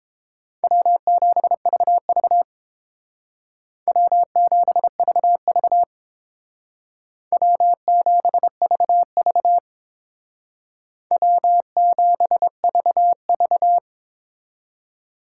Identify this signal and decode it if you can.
Morse code, heard as a keyed tone, about 28 wpm, 700 Hz. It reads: W7VV W7VV W7VV W7VV